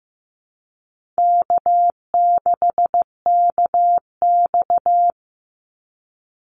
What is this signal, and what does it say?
Morse code, heard as a keyed tone, about 15 wpm, 700 Hz. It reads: K6KX
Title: Easy Call Signs Repeated 3x - 15wpm